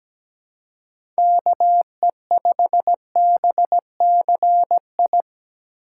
Morse code, heard as a keyed tone, about 17 wpm, 700 Hz. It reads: KE5BCI
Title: USA Call Signs - 6 Characters - 4x Character Spacing - 17wpm